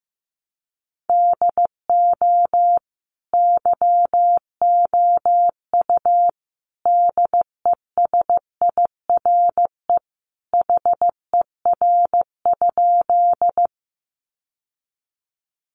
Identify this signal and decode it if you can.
Morse code, heard as a keyed tone, about 15 wpm, 700 Hz. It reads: DO YOU DESIRE HER?
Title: Sentences from Top 1100 Words 15wpm